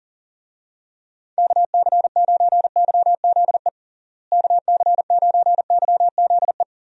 Morse code, heard as a keyed tone, about 40 wpm, 700 Hz. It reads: KC9YZE KC9YZE
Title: USA Call Signs - 6 Characters - Repeated 3x - 40wpm